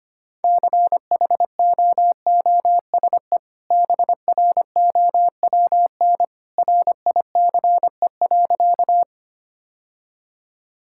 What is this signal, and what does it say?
Morse code, heard as a keyed tone, about 25 wpm, 700 Hz. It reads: CHOOSE BROWN RICE.
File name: Easy Sentences with Hints - 25wpm